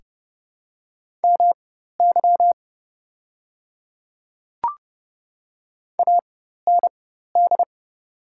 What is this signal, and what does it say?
Morse code, heard as a keyed tone, about 30 wpm, spaced as for 14 wpm, 700 Hz. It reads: MY AND
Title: 4x Character Spacing - Top 200 Words - 30wpm